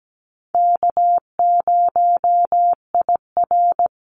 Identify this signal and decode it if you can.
Morse code, heard as a keyed tone, about 17 wpm, 700 Hz. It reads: K0IR
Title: Easy Call Signs Repeated 2x - 17wpm